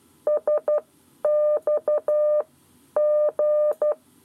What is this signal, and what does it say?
Morse code, heard as a keyed tone, about 12 wpm, spaced as for 10 wpm, 600 Hz. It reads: SX G
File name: 7 Lezione cw 1